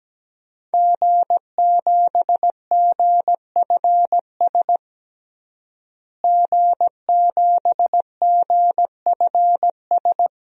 Morse code, heard as a keyed tone, about 17 wpm, 700 Hz. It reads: G7GFS G7GFS
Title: DXCC-100 Call Signs - 5 Characters - Repeated 2x - 17wpm